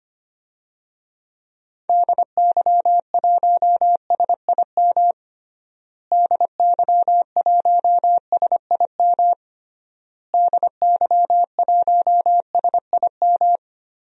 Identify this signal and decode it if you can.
Morse code, heard as a keyed tone, about 25 wpm, 700 Hz. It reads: DY1SIM DY1SIM DY1SIM
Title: DXCC-100 Call Signs - 6 Characters - Repeated 3x - 25wpm